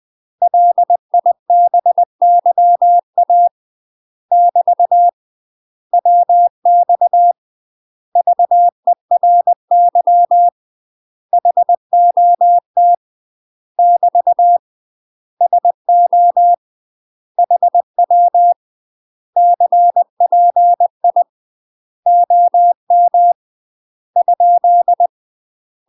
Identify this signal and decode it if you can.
Morse code, heard as a keyed tone, about 20 wpm, 700 Hz. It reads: LIBYA <BT> WX VERY HOT <BT> SO HW CPI OM ?
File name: UK MASTER QSOs TRAINING / 20WPM / UK MASTER QSOs 0047